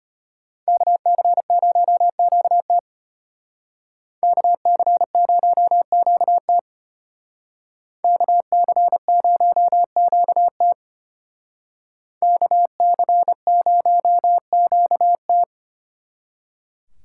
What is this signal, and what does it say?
Morse code, heard as a keyed tone, about 32 wpm, 700 Hz. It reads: KC0QT KC0QT KC0QT KC0QT